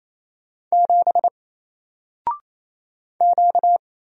Morse code, heard as a keyed tone, about 29 wpm, 700 Hz. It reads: 7 Q
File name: Single-Letter-Number-28wpm